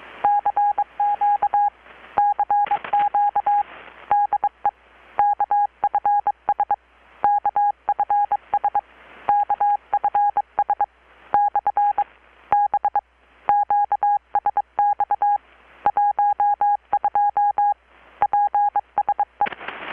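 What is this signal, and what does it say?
Morse code, heard as a keyed tone, about 22 wpm, spaced as for 20 wpm, 820 Hz. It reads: CQ CQ DE KFS KFS KFS/B QSX 12 PSE
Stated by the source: Morse from KPH's Night of Nights XX maritime radiotelegraphy event, received in Chiapas, Mexico about 3,800 km away on a Kenwood R-600 with a grounded 10 m long-wire antenna.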